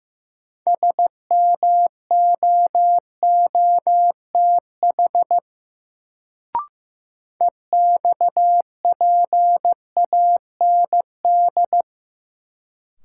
Morse code, heard as a keyed tone, about 15 wpm, 700 Hz. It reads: SMOOTH EXPAND
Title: Sentences from Top 1400 Words - New Word Drill - 15wpm